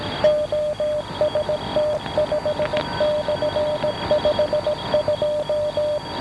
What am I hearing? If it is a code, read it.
Morse code, heard as a keyed tone, about 17 wpm, 600 Hz, with noise. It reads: OST5/52
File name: OST5 12781.5 1131